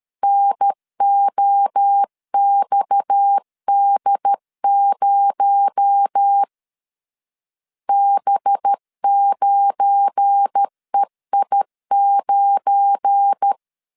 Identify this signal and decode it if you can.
Morse code, heard as a keyed tone, about 13 wpm, 800 Hz. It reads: NOXD0 B9EI9
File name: LEZIONI / 7 GF9Z / 7 GF9 12wpm 2319